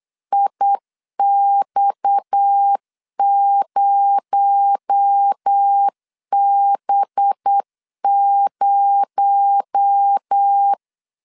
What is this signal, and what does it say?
Morse code, heard as a keyed tone, about 8 wpm, 800 Hz. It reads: IX0B0
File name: LEZIONI / 6 PKX0 / 6 PKX0 8wpm 3341